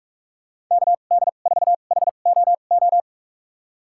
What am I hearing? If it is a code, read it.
Morse code, heard as a keyed tone, about 46 wpm, spaced as for 32 wpm, 700 Hz. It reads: KD4HOO
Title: USA Call Signs - 2x Character Spacing - 45wpm